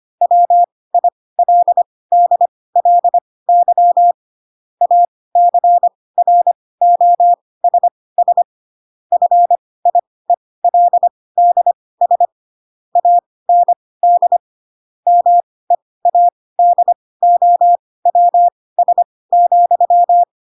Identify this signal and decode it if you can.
Morse code, heard as a keyed tone, about 25 wpm, spaced as for 18 wpm, 700 Hz. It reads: WILDLY ACROSS FIELDS AND MEADOWS,
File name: Pinocchio / Pinocchio ch4 12 25 ik1hge